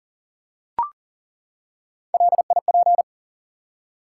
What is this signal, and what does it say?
Morse code, heard as a keyed tone, about 40 wpm, 700 Hz. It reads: LIP
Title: Three Letter Words - 40wpm